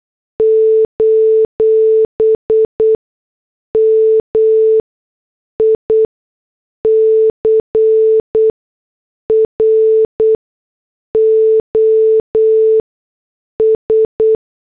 Morse code, heard as a keyed tone, about 8 wpm, spaced as for 6 wpm, 440 Hz. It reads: : MICROS
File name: morse / headlines